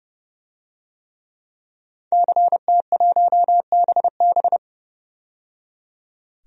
Morse code, heard as a keyed tone, about 30 wpm, 700 Hz. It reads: CT1BB
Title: DXCC-100 Call Signs - 5 Characters - 30wpm